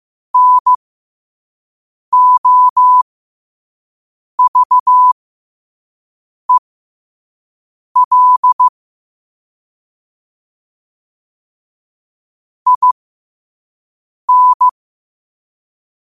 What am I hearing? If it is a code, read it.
Morse code, heard as a keyed tone, about 15 wpm, spaced as for 5 wpm, 1000 Hz. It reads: NOVEL IN
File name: qotdfiles / 5wpm